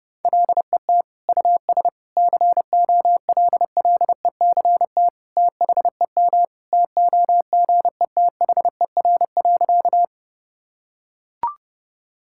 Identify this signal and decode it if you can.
Morse code, heard as a keyed tone, about 30 wpm, 700 Hz. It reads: LET US COLLECT THEM TOGETHER.
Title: Sentences from Top 1600 Words 30wpm